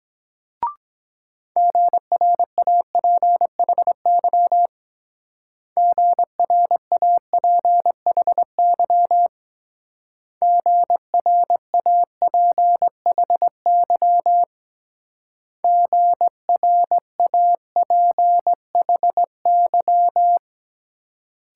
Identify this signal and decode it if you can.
Morse code, heard as a keyed tone, about 21 wpm, 700 Hz. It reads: GRAPHY GRAPHY GRAPHY GRAPHY
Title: Speed-Racing Multi-letter Phonemes - 17wpm